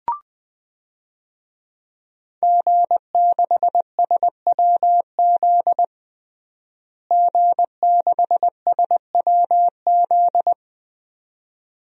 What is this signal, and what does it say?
Morse code, heard as a keyed tone, about 20 wpm, 700 Hz. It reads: G6SWZ G6SWZ
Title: DXCC-100 Call Signs - 5 Characters - Repeated 2x - 20wpm